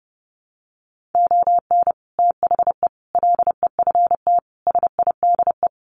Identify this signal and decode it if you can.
Morse code, heard as a keyed tone, about 30 wpm, 700 Hz. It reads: ON THE LEFT SIDE
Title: Sets of 5 Words Encore2 30wpm